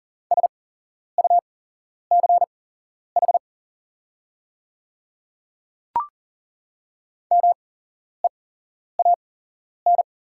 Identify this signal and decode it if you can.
Morse code, heard as a keyed tone, about 40 wpm, spaced as for 11 wpm, 690 Hz. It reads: SUCH MEAN
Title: 8x Character Spacing - Top 200 Words - 40wpm